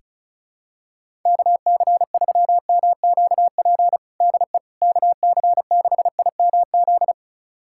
Morse code, heard as a keyed tone, about 36 wpm, 700 Hz. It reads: KC3MQP DE KC6IMZ
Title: USA Callsign DE Callsign 35wpm